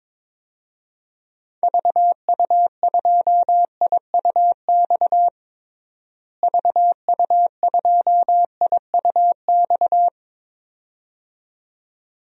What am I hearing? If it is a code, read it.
Morse code, heard as a keyed tone, about 22 wpm, 700 Hz. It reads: VU2IUX VU2IUX
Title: DXCC-100 Call Signs - 6 Characters - Repeated 2x - 22wpm